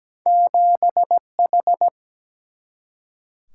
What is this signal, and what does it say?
Morse code, heard as a keyed tone, about 17 wpm, 700 Hz. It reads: 7H
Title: Single-Letter Single-Number 17wpm